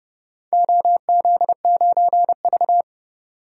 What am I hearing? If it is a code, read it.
Morse code, heard as a keyed tone, about 29 wpm, 700 Hz. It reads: OZ9V
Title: DXCC-100 Call Signs - 4 Characters - 30wpm